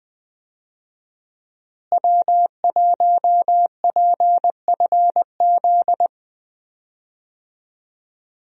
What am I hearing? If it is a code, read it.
Morse code, heard as a keyed tone, about 20 wpm, 700 Hz. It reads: W1PFZ
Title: USA Call Signs - 5 Characters - Repeated 3x - 20wpm